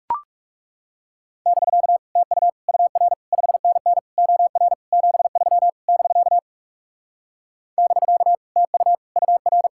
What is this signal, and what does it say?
Morse code, heard as a keyed tone, about 39 wpm, 700 Hz. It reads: <BK> TU UR 5NN OR 73 <BK> <BK> TU UR